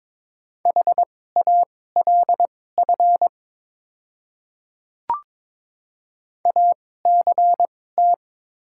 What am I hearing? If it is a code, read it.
Morse code, heard as a keyed tone, about 22 wpm, spaced as for 16 wpm, 700 Hz. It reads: HALF ACT
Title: 2x Character Spacing - Top 500 Words - 22wpm